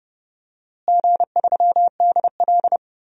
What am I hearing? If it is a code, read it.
Morse code, heard as a keyed tone, about 31 wpm, 700 Hz. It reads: G3DL
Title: DXCC-100 Call Signs - 4 Characters - Repeated 3x - 30wpm